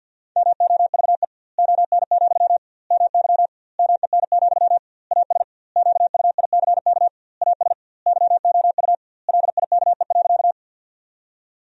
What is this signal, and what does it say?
Morse code, heard as a keyed tone, about 51 wpm, 700 Hz. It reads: MOVE ON, MY MEN, AS QUICK AS YOU LIKE.
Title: 2x Word-Spacing - Sentences from Top 500 Words - 50wpm